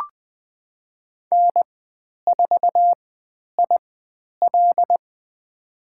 Morse code, heard as a keyed tone, about 20 wpm, spaced as for 10 wpm, 700 Hz. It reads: N4IL
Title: 102 Lesson 17g - Letter u - Call Signs 20wpm